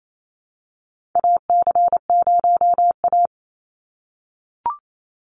Morse code, heard as a keyed tone, about 28 wpm, 700 Hz. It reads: AC0A
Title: NA SOTA Chaser Callsigns 28wpm